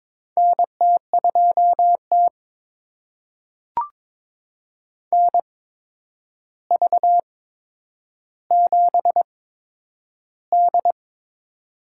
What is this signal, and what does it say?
Morse code, heard as a keyed tone, about 22 wpm, spaced as for 9 wpm, 700 Hz. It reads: NT2T NV7D
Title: USA Call Signs - 4 Characters - 8x Character Spacing - 22wpm